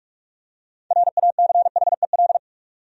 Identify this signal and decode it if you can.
Morse code, heard as a keyed tone, about 43 wpm, 700 Hz. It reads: RACHEL
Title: USA Common Names - 45wpm